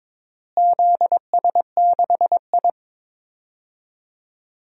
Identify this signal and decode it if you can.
Morse code, heard as a keyed tone, about 21 wpm, 700 Hz. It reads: ZS6I
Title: DXCC-100 Call Signs - 4 Characters - Repeated 3x - 22wpm